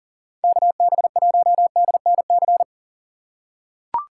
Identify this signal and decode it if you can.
Morse code, heard as a keyed tone, about 40 wpm, 700 Hz. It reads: KB1DNC
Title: USA Call Signs - 2x Character Spacing - 40wpm